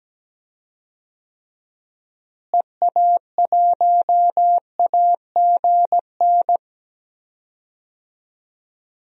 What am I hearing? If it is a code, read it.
Morse code, heard as a keyed tone, about 17 wpm, 700 Hz. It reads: EA1AGN
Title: DXCC-100 Call Signs - 6 Characters - Repeated 3x - 17wpm